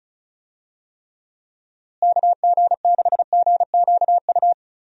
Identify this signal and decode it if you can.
Morse code, heard as a keyed tone, about 34 wpm, 700 Hz. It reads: KG6GQU
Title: USA Call Signs - 6 Characters - Repeated 3x - 35wpm